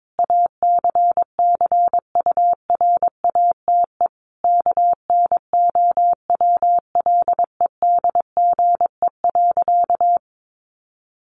Morse code, heard as a keyed tone, about 22 wpm, 700 Hz. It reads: ACCURATE KNOWLEDGE.
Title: Sentences from Top 1200 Words 22wpm